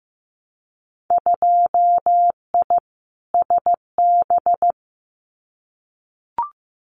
Morse code, heard as a keyed tone, about 15 wpm, 700 Hz. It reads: 2I SB
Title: Winter Field Day Exchanges Encore 15wpm